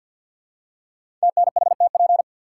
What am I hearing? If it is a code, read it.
Morse code, heard as a keyed tone, about 50 wpm, 700 Hz. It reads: TNHTP
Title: Code Groups - 5 Letters - 50wpm